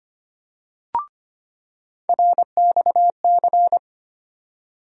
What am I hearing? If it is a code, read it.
Morse code, heard as a keyed tone, about 25 wpm, 700 Hz. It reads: RXC
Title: USA Call Sign Postfixes - 3 Letters 25wpm